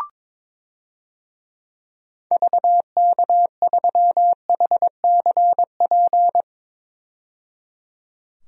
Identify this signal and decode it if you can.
Morse code, heard as a keyed tone, about 22 wpm, 700 Hz. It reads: VK3HCP
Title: DXCC-100 Call Signs - 6 Characters - 22wpm